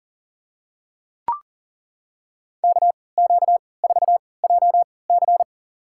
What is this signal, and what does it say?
Morse code, heard as a keyed tone, about 40 wpm, spaced as for 23 wpm, 700 Hz. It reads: KQ4JC
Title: USA Call Signs - 5 Characters - 3x Character Spacing - 40wpm